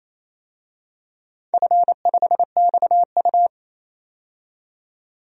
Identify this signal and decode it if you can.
Morse code, heard as a keyed tone, about 28 wpm, 700 Hz. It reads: F5XU
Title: DXCC-100 Call Signs - 4 Characters - 28wpm